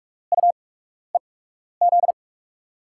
Morse code, heard as a keyed tone, about 46 wpm, spaced as for 12 wpm, 700 Hz. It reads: UEZ
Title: USA Call Signs - 6 Characters - 8x Character Spacing - 45wpm